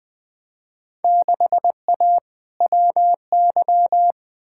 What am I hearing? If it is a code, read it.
Morse code, heard as a keyed tone, about 20 wpm, 700 Hz. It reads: 6A WY
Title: Field Day Exchanges Encore - 20wpm